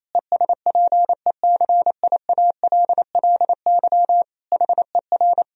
Morse code, heard as a keyed tone, about 27 wpm, 700 Hz. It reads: ESPECIALLY HER
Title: Sentences from Top 1000 Words - Review Part 4 - 28wpm